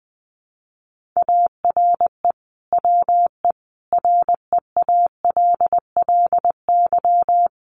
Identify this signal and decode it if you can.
Morse code, heard as a keyed tone, about 20 wpm, 700 Hz. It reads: ARE WE REALLY
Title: Sentences from Top 1400 Words 20wpm